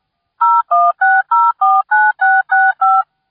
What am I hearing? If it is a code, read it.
DTMF, heard as telephone touch tones, tones of about 220 ms. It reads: 016049665